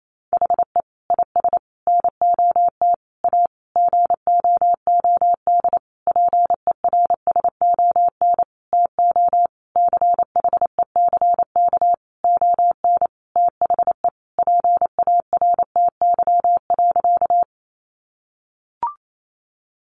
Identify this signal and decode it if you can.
Morse code, heard as a keyed tone, about 28 wpm, 700 Hz. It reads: HE IS NOT A GOOD PERSON TO CHECK ON THE PARTY.
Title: Sentences from Top 1400 Words 28wpm